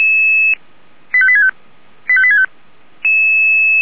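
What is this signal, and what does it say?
Five-tone selective call (ZVEI), heard as quick runs of five tones, tones of about 70 ms. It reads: 85865 85865
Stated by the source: ZVEI five-tone call 85861 received by SDR.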